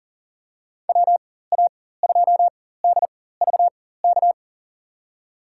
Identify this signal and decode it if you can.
Morse code, heard as a keyed tone, about 40 wpm, spaced as for 19 wpm, 700 Hz. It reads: WA2DVK
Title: USA Call Signs - 6 Characters - 4x Character Spacing - 40wpm